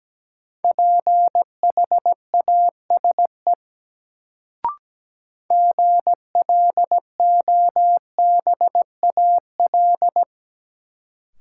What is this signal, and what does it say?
Morse code, heard as a keyed tone, about 17 wpm, 700 Hz. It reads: PHASE GLOBAL
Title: Sentences from Top 1200 Words - New Word Drill - 17wpm